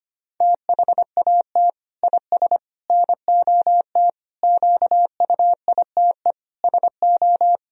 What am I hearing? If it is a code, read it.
Morse code, heard as a keyed tone, about 25 wpm, 700 Hz. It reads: THAT IS NOT QUITE SO
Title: Sentences from Top 1000 Words - Review Part 3 - 25wpm